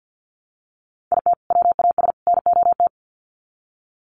Morse code, heard as a keyed tone, about 50 wpm, 700 Hz. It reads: IT WAS NOT
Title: Sets of 3 Words 50wpm